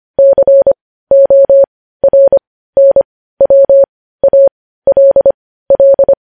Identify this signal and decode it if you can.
Morse code, heard as a keyed tone, about 25 wpm, spaced as for 15 wpm, 550 Hz. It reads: CORNWALL
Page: news-headlines-in-morse-code-at-15-wpm